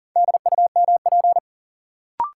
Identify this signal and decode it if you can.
Morse code, heard as a keyed tone, about 37 wpm, 700 Hz. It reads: DUMP
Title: Sentences from Top 1300 Words - New Word Drill Alt - 40wpm